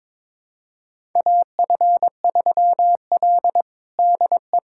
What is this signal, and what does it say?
Morse code, heard as a keyed tone, about 22 wpm, 700 Hz. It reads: AF3L DE
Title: USA Callsign DE Callsign - 4 Characters - 22wpm